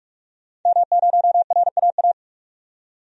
Morse code, heard as a keyed tone, about 41 wpm, 700 Hz. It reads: M0RAA
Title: DXCC-100 Call Signs - 5 Characters - 45wpm